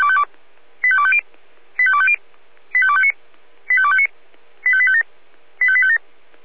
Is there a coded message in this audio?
Yes: five-tone selective call (ZVEI) from 0.8 to 6.0 s, heard as quick runs of five tones, tones of about 70 ms. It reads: 85250 85250 85259 85259 85857 85857